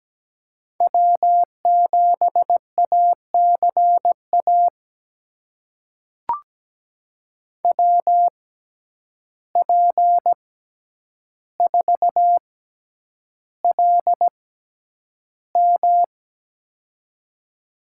Morse code, heard as a keyed tone, about 17 wpm, spaced as for 9 wpm, 700 Hz. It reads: W7ACA WP4LM
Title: USA Call Signs - 5 Characters - 6x Character Spacing - 17wpm